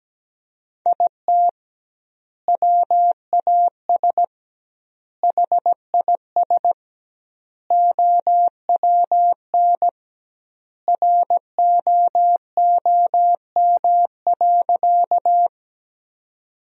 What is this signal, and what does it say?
Morse code, heard as a keyed tone, about 17 wpm, 700 Hz. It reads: IT WAS HIS OWN ROOM.
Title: 2x Word-Spacing - Sentences from Top 300 Words - 17wpm